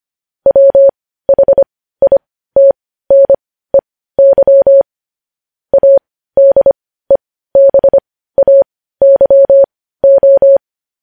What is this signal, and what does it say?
Morse code, heard as a keyed tone, about 25 wpm, spaced as for 15 wpm, 550 Hz. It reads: WHITNEY ADEBAYO
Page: News Headlines in Morse Code at 15 WPM